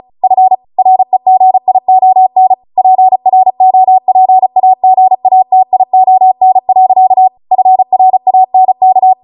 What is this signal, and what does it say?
Morse code, heard as a keyed tone, about 34 wpm, 750 Hz. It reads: F REGION PROPAGATION. FRANK